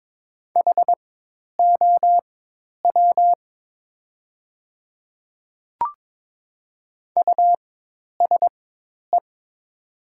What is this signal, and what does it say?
Morse code, heard as a keyed tone, about 22 wpm, spaced as for 10 wpm, 700 Hz. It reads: HOW USE
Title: 4x Character Spacing - Top 100 Words - 22wpm